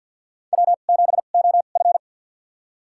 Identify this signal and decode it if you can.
Morse code, heard as a keyed tone, about 53 wpm, spaced as for 37 wpm, 700 Hz. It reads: W7OF
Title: USA Call Signs - 4 Characters - 2x Character Spacing - 50wpm